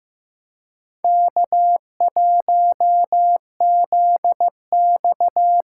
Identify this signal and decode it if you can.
Morse code, heard as a keyed tone, about 15 wpm, 700 Hz. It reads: K1ZX
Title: USA Call Signs - 6x Character Spacing - 15wpm